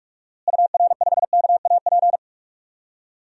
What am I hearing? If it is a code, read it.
Morse code, heard as a keyed tone, about 46 wpm, 700 Hz. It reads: UR5KAP
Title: DXCC-100 Call Signs - 6 Characters - Repeated 3x - 45wpm